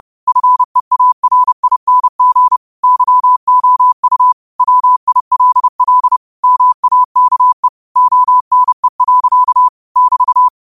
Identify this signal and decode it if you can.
Morse code, heard as a keyed tone, about 30 wpm, 1000 Hz. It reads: FEARING YOU WILL MAKE ONE. <BT>
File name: qotdfiles / 30wpm